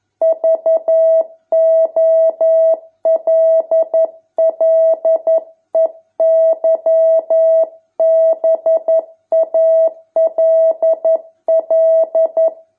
Morse code, heard as a keyed tone, about 11 wpm, 650 Hz. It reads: VOLLEYBALL